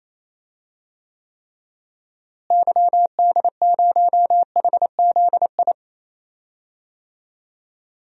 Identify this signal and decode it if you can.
Morse code, heard as a keyed tone, about 28 wpm, 700 Hz. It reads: YD0HZI
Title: DXCC-100 Call Signs - 6 Characters - Repeated 3x - 28wpm